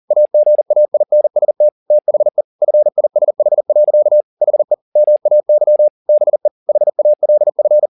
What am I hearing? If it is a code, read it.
Morse code, heard as a keyed tone, about 40 wpm, 600 Hz. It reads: AGAINST THE FISH. HE MAY BE HALF
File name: the old man and the sea 2 40wpm